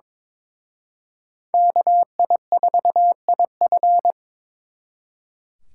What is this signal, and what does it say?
Morse code, heard as a keyed tone, about 26 wpm, 700 Hz. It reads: KI4IF
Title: Speed Racing - USA 2x2 Call Signs 22wpm